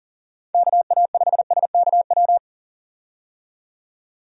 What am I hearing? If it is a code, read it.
Morse code, heard as a keyed tone, about 40 wpm, 700 Hz. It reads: KA5SKW
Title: USA Call Signs - 6 Characters - 40wpm